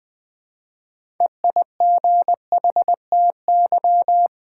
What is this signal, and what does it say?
Morse code, heard as a keyed tone, about 20 wpm, 700 Hz. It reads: EIGHTY
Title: Build A Word - 3 Letters At A Time - 6-Letter Words - 20wpm